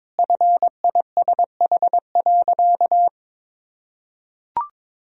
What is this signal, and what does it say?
Morse code, heard as a keyed tone, about 22 wpm, 700 Hz. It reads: FISH.
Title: Sentences from Top 1000 Words - Review Part 2 - 22wpm